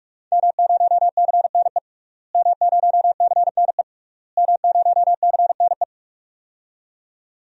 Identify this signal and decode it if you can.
Morse code, heard as a keyed tone, about 42 wpm, 700 Hz. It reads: M0CNE M0CNE M0CNE
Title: DXCC-100 Call Signs - 5 Characters - Repeated 3x - 45wpm